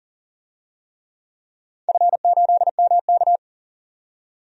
Morse code, heard as a keyed tone, about 38 wpm, 700 Hz. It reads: F8MK
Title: DXCC-100 Call Signs - 4 Characters - 40wpm